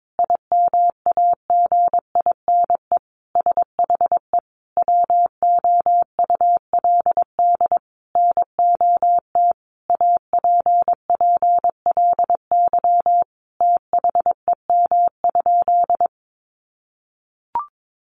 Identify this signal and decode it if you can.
Morse code, heard as a keyed tone, about 22 wpm, 700 Hz. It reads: IMAGINE SHE WOULD NOT APPLY THEM?
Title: Sentences from Top 1400 Words 22wpm